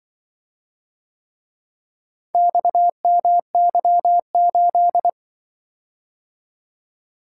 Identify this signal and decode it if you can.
Morse code, heard as a keyed tone, about 25 wpm, 700 Hz. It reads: XMY8
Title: Speed-Racing - Four Letter-Number Combinations - 20wpm